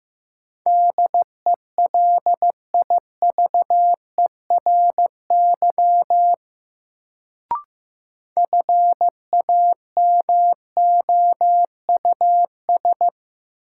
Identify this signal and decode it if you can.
Morse code, heard as a keyed tone, about 16 wpm, 700 Hz. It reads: DELIVERY FAMOUS
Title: Sentences from Top 1500 Words - New Word Drill Alt - 15wpm